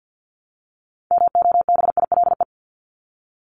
Morse code, heard as a keyed tone, about 50 wpm, 700 Hz. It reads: MOBILE
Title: Top 1751-2000 Words 50wpm